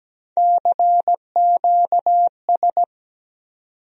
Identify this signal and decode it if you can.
Morse code, heard as a keyed tone, about 17 wpm, 700 Hz. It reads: CQS
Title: USA Call Sign Postfixes - 3 Letters 17wpm